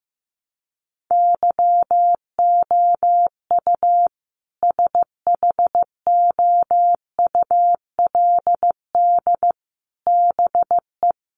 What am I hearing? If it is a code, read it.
Morse code, heard as a keyed tone, about 15 wpm, 700 Hz. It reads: YOU SHOULD BE
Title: Sets of 5 Words Encore 15wpm